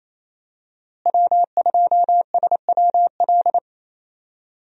Morse code, heard as a keyed tone, about 28 wpm, 700 Hz. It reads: W2SWL
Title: USA Call Signs - 5 Characters - Repeated 2x - 28wpm